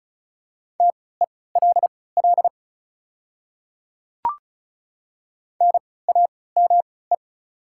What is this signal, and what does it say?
Morse code, heard as a keyed tone, about 35 wpm, spaced as for 20 wpm, 690 Hz. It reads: TELL NAME
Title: 3x Character Spacing - Top 200 Words - 35wpm